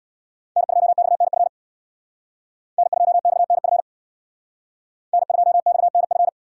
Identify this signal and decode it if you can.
Morse code, heard as a keyed tone, about 63 wpm, 700 Hz. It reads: N2XNF N2XNF N2XNF